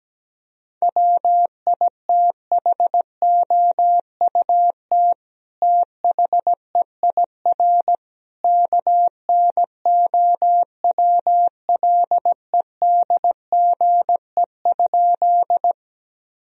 Morse code, heard as a keyed tone, about 17 wpm, 700 Hz. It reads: WITHOUT THEIR KNOWLEDGE?
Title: Sentences from Top 1000 Words - Review Part 3 - 17wpm